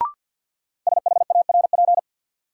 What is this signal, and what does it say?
Morse code, heard as a keyed tone, about 52 wpm, 700 Hz. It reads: SHARP
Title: Sentences from Top 1600 Words - New Word Drill Alt - 50wpm